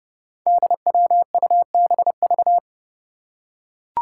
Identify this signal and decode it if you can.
Morse code, heard as a keyed tone, about 30 wpm, 700 Hz. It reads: DWUBV
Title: Five Challenging Letter-Number Combinations 30wpm